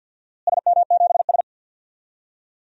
Morse code, heard as a keyed tone, about 52 wpm, 700 Hz. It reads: SM7S
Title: DXCC-100 Call Signs - 4 Characters - Repeated 2x - 50wpm